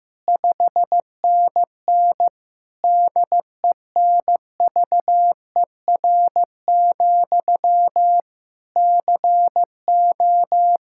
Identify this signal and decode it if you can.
Morse code, heard as a keyed tone, about 15 wpm, 700 Hz. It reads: 5NN DENVER, CO
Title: Practice QSO - Exchange 1 - Beginning - 15wpm